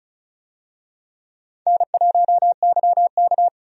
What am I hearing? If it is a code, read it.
Morse code, heard as a keyed tone, about 34 wpm, 700 Hz. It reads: N1YK
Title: USA Call Signs - 4 Characters - 35wpm